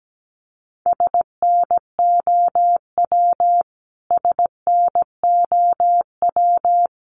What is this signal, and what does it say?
Morse code, heard as a keyed tone, about 17 wpm, 700 Hz. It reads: SNOW SNOW
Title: Top 500 Words Repeated 17wpm